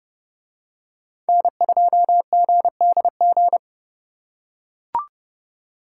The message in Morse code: N2GDG

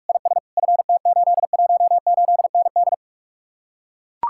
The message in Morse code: IS FT818ND

